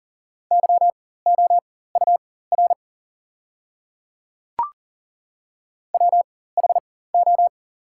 Morse code, heard as a keyed tone, about 38 wpm, spaced as for 18 wpm, 690 Hz. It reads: YOUR WHO
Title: 4x Character Spacing - Top 100 Words - 40wpm